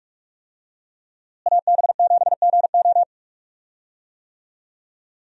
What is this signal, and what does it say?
Morse code, heard as a keyed tone, about 44 wpm, 700 Hz. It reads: AB7GO
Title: USA Call Signs - 5 Characters - Repeated 2x - 45wpm